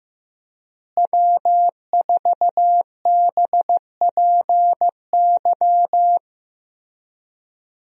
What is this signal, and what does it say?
Morse code, heard as a keyed tone, about 15 wpm, 700 Hz. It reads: W4BPY
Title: USA Call Signs - 5 Characters - Repeated 3x - 15wpm